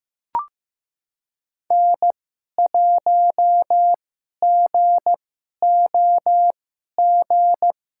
Morse code, heard as a keyed tone, about 15 wpm, spaced as for 11 wpm, 700 Hz. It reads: N1GOG
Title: USA Call Signs - 2x Character Spacing - 15wpm